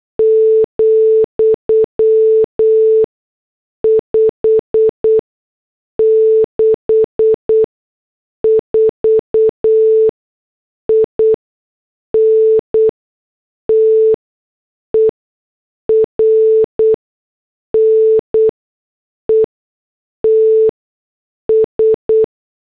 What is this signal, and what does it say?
Morse code, heard as a keyed tone, about 8 wpm, 440 Hz. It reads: , 5 6 4 I N T E R N E T S
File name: morse / headlines